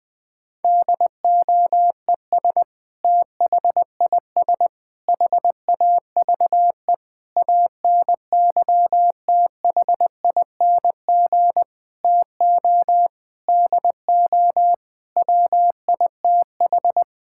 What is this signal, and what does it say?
Morse code, heard as a keyed tone, about 20 wpm, 700 Hz. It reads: DOES THIS HAVE ANYTHING TO DO WITH
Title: Sentences with 5-Word Phrases - 20wpm